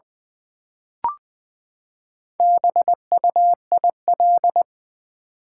Morse code, heard as a keyed tone, about 20 wpm, 700 Hz. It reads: BUIL
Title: Build A Word - 4 Letters At A Time - 8-Letter Words - 20wpm